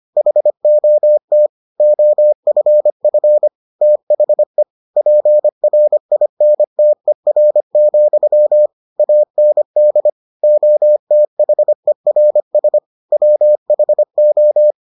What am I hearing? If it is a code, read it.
Morse code, heard as a keyed tone, about 25 wpm, 600 Hz. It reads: HOT OFF THE PRINTER, AND OTHERS WHO